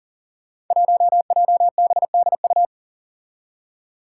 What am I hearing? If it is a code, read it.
Morse code, heard as a keyed tone, about 40 wpm, 700 Hz. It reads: 1JBDU
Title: Five Challenging Letter-Number Combinations 40wpm